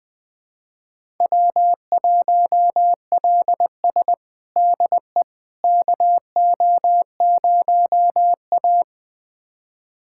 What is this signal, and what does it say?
Morse code, heard as a keyed tone, about 21 wpm, 700 Hz. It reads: W1LS DE KO0A
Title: USA Callsign DE Callsign - 4 Characters - 20wpm